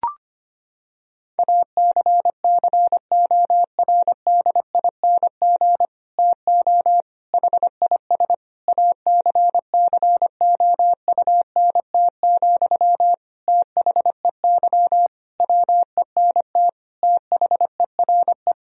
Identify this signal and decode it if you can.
Morse code, heard as a keyed tone, about 25 wpm, 700 Hz. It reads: ACCORDING TO HIS ACCOUNT, THEY WENT THERE
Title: Sentences from Top 1000 Words - Review Part 4 - 25wpm